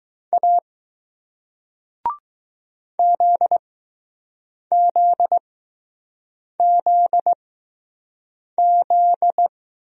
Morse code, heard as a keyed tone, about 20 wpm, spaced as for 6 wpm, 700 Hz. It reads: A ZZZZ